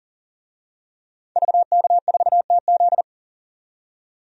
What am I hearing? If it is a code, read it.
Morse code, heard as a keyed tone, about 40 wpm, 700 Hz. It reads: VK4TZ
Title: DXCC-100 Call Signs - 5 Characters - 40wpm